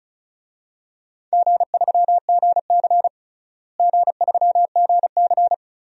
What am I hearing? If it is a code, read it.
Morse code, heard as a keyed tone, about 34 wpm, 700 Hz. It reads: G3GC G3GC
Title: DXCC-100 Call Signs - 4 Characters - Repeated 3x - 35wpm